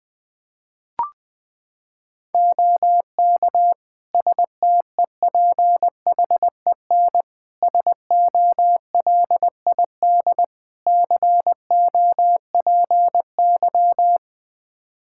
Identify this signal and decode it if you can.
Morse code, heard as a keyed tone, about 21 wpm, 700 Hz. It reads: OK STEPHEN SOLID COPY
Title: K1USN SST QSO Practice 20wpm